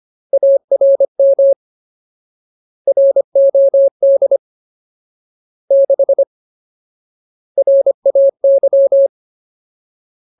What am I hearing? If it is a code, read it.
Morse code, heard as a keyed tone, about 25 wpm, 550 Hz. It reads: ARM ROD 6 RAY
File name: Ultimate Word List / 25WPM Ultimate